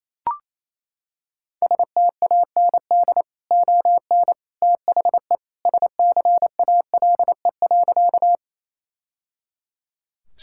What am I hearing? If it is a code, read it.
Morse code, heard as a keyed tone, about 27 wpm, 700 Hz. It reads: STAND ON THE SCALE.
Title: Sentences from Top 1000 Words - Review Part 5 - 28wpm